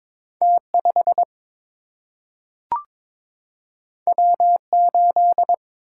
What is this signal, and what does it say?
Morse code, heard as a keyed tone, about 22 wpm, 700 Hz. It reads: T 5 W 8
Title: Single-Letter Single-Number 22wpm